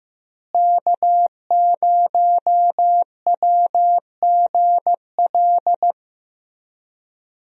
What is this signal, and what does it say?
Morse code, heard as a keyed tone, about 15 wpm, 700 Hz. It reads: K0WGL
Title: USA Call Signs - 5 Characters - Repeated 3x - 15wpm